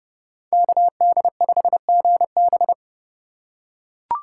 Morse code, heard as a keyed tone, about 30 wpm, 700 Hz. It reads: KD5GB